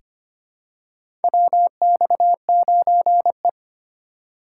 Morse code, heard as a keyed tone, about 25 wpm, 700 Hz. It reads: WX9E
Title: USA Call Signs - 4x Character Spacing - 25wpm